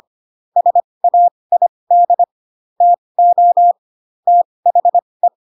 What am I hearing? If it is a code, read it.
Morse code, heard as a keyed tone, about 25 wpm, spaced as for 20 wpm, 700 Hz. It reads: SAID TO THE
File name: Pinocchio / Pinocchio ch8 15 25 ik1hge